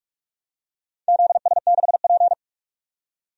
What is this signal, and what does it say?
Morse code, heard as a keyed tone, about 44 wpm, 700 Hz. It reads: ZS6P
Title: DXCC-100 Call Signs - 4 Characters - 45wpm